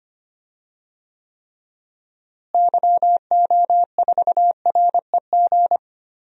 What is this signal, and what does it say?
Morse code, heard as a keyed tone, about 25 wpm, 700 Hz. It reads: YO4REG
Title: DXCC-100 Call Signs - 6 Characters - Repeated 3x - 25wpm